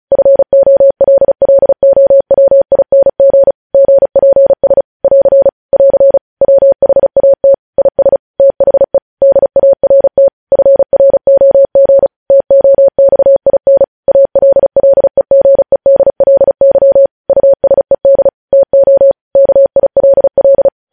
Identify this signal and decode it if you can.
Morse code, heard as a keyed tone, about 36 wpm, 550 Hz. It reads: FOLLOWING GPS <AR> <AR> WHAT IS THE DART FROG TOXIN ALLEGEDLY USED TO KILL